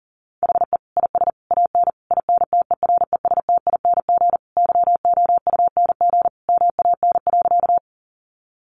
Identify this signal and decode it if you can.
Morse code, heard as a keyed tone, about 40 wpm, 700 Hz. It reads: HE IS AN INTERESTING YOUNG MAN.
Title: Sentences from Top 1000 Words 40wpm